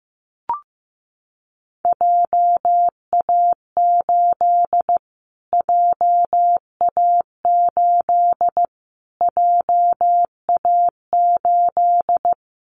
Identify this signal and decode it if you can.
Morse code, heard as a keyed tone, about 15 wpm, 700 Hz. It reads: JA8 JA8 JA8
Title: Callsign Prefixes for Top 25 DXCC - Repeated 3x - 15wpm